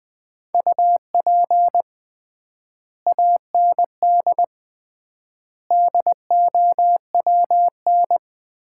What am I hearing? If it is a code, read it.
Morse code, heard as a keyed tone, about 21 wpm, 700 Hz. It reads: UP AND DOWN
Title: 3x Word-Spacing - Sets of 4 Words - 20wpm